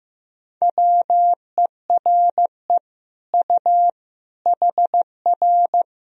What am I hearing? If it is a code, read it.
Morse code, heard as a keyed tone, about 15 wpm, 700 Hz. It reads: WERE U HR
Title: Sentences from Top 200 Words and Abbreviations 15wpm